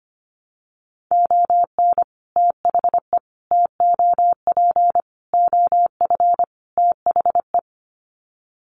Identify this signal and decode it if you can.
Morse code, heard as a keyed tone, about 25 wpm, 700 Hz. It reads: ON THE TOP OF THE
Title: Sets of 5 Words Encore 25wpm